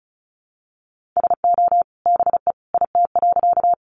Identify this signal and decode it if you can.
Morse code, heard as a keyed tone, about 34 wpm, 700 Hz. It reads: SO BE IT.
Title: Sentences from Top 100 Words 35wpm